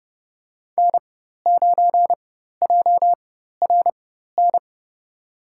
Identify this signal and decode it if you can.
Morse code, heard as a keyed tone, about 32 wpm, spaced as for 14 wpm, 700 Hz. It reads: N9JRN
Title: USA Call Signs - 5 Characters - 4x Character Spacing - 30wpm